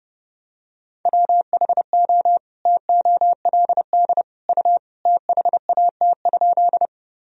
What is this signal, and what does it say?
Morse code, heard as a keyed tone, about 30 wpm, 700 Hz. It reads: WHO TOLD U THAT?
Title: Sentences from Top 400 Words and Abbreviations 30wpm